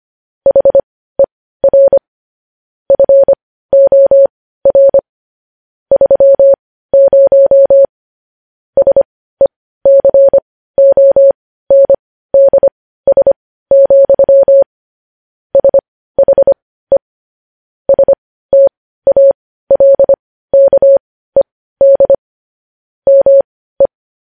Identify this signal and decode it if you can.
Morse code, heard as a keyed tone, about 25 wpm, spaced as for 15 wpm, 550 Hz. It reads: HER FOR 30 SECONDS, SHE STALKED ME
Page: news-headlines-in-morse-code-at-15-wpm